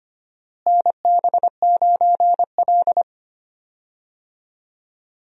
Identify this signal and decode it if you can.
Morse code, heard as a keyed tone, about 25 wpm, 700 Hz. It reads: NB9L
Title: USA Call Signs - 4 Characters - Repeated 2x - 25wpm